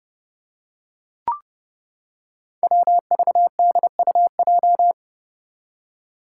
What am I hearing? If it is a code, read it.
Morse code, heard as a keyed tone, about 30 wpm, 700 Hz. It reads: WVDUJ
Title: Five Challenging Letter-Number Combinations 30wpm